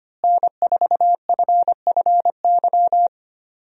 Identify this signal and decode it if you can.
Morse code, heard as a keyed tone, about 25 wpm, 700 Hz. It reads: N4FFY
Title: USA Call Signs - 5 Characters - Repeated 3x - 25wpm